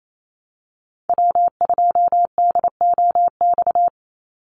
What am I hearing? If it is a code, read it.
Morse code, heard as a keyed tone, about 28 wpm, 700 Hz. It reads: W2DOX
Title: USA Call Signs 28wpm